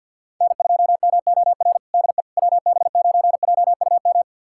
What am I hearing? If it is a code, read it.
Morse code, heard as a keyed tone, about 50 wpm, 700 Hz. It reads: N2MOR DE WB9JUM
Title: USA Callsign DE Callsign 50wpm